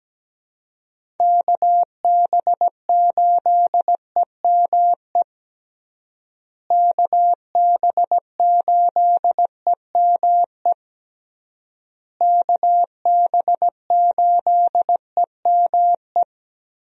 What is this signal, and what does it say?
Morse code, heard as a keyed tone, about 17 wpm, 700 Hz. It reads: KB8EME KB8EME KB8EME
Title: USA Call Signs - 6 Characters - Repeated 3x - 17wpm